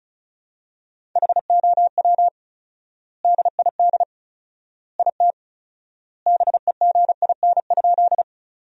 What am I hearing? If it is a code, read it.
Morse code, heard as a keyed tone, about 34 wpm, 700 Hz. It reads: HOW DID IT BEGIN?
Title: 4x Word-Spacing - Sentences from Top 300 Words - 35wpm